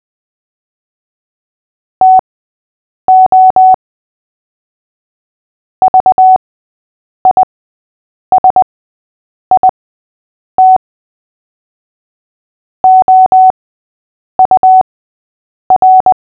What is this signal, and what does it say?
Morse code, heard as a keyed tone, about 20 wpm, spaced as for 8 wpm, 750 Hz. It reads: TO VISIT OUR
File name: cw / fists / Short-Story-5-8wpm